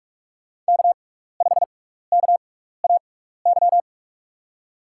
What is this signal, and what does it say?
Morse code, heard as a keyed tone, about 44 wpm, spaced as for 15 wpm, 700 Hz. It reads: K5KAY
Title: USA Call Signs - 5 Characters - 6x Character Spacing - 45wpm